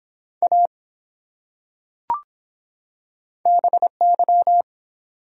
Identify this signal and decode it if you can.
Morse code, heard as a keyed tone, about 24 wpm, 700 Hz. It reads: A BY